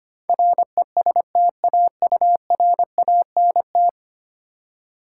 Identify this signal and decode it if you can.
Morse code, heard as a keyed tone, about 28 wpm, 700 Hz. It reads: RESTAURANT
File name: Easy Words - Speed-Racing with Hints - 25wpm